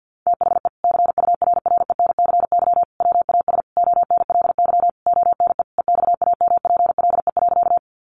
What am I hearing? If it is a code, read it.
Morse code, heard as a keyed tone, about 50 wpm, 700 Hz. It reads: THE CURRENCY WAS ONLY ONE EXAMPLE.
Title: Sentences from Top 1500 Words 50wpm